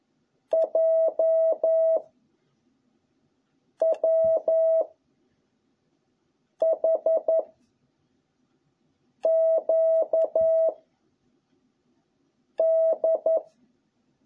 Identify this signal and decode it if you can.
Morse code, heard as a keyed tone, about 11 wpm, 660 Hz. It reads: J W H Q D